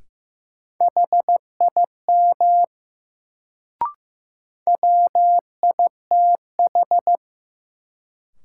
Morse code, heard as a keyed tone, about 15 wpm, 700 Hz. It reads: HIM WITH
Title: Sets of 2 Words - New Word Drill - 15wpm